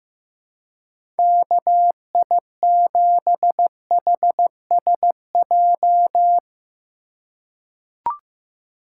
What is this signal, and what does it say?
Morse code, heard as a keyed tone, about 15 wpm, 700 Hz. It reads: KI7HSJ
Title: USA Call Signs - 6 Characters - 4x Character Spacing - 15wpm